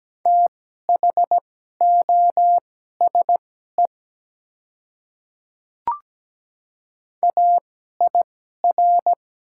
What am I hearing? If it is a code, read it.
Morse code, heard as a keyed tone, about 17 wpm, spaced as for 12 wpm, 700 Hz. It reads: THOSE AIR
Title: 2x Character Spacing - Top 300 Words - 17wpm